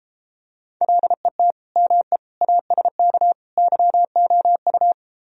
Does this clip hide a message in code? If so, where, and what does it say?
Yes: Morse code from 0.8 to 4.9 s, heard as a keyed tone, about 33 wpm, 700 Hz. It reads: LET ME ASK YOU